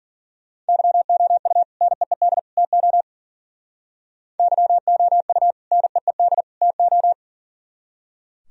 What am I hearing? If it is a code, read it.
Morse code, heard as a keyed tone, about 51 wpm, 700 Hz. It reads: YOU NEED TO YOU NEED TO